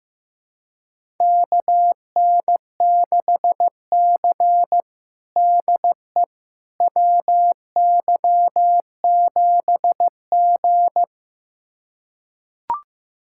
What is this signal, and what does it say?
Morse code, heard as a keyed tone, about 15 wpm, 700 Hz. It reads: KN6C DE WY7G
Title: USA Callsign DE Callsign - 4 Characters - 15wpm